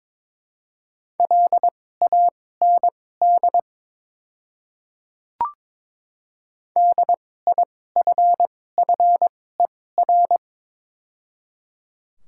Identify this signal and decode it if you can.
Morse code, heard as a keyed tone, about 23 wpm, spaced as for 16 wpm, 700 Hz. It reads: LAND DIFFER
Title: 2x Character Spacing - Top 500 Words - 22wpm